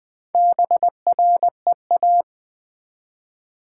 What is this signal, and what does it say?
Morse code, heard as a keyed tone, about 20 wpm, 700 Hz. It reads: BREA
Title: Build A Word - 1 Letter At A Time - 5-Letter Words - 20wpm